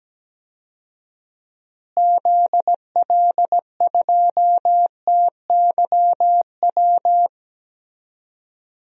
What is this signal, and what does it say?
Morse code, heard as a keyed tone, about 17 wpm, 700 Hz. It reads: ZL2TYW
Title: DXCC-100 Call Signs - 6 Characters - Repeated 2x - 17wpm